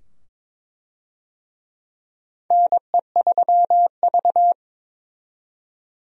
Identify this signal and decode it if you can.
Morse code, heard as a keyed tone, about 22 wpm, 700 Hz. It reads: NE3V
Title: USA Call Signs - 4 Characters - 22wpm